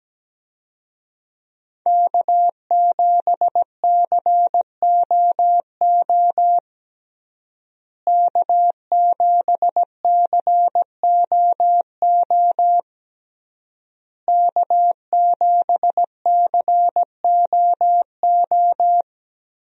Morse code, heard as a keyed tone, about 17 wpm, 700 Hz. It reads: K7COO K7COO K7COO
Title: USA Call Signs - 5 Characters - Repeated 3x - 17wpm